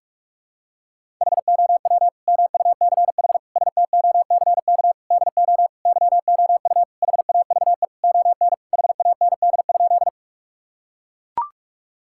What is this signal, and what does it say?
Morse code, heard as a keyed tone, about 41 wpm, 700 Hz. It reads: HOW MUCH STOCK DO YOU HAVE ON HAND?
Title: Sentences from Top 1000 Words - Review Part 5 - 45wpm